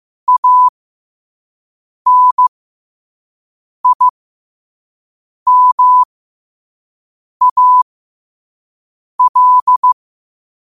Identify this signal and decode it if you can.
Morse code, heard as a keyed tone, about 15 wpm, spaced as for 5 wpm, 1000 Hz. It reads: ANIMAL
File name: qotdfiles / 5wpm